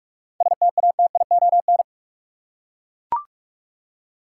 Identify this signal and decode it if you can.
Morse code, heard as a keyed tone, about 41 wpm, 700 Hz. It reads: STATION
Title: Seven Letter Words - 45wpm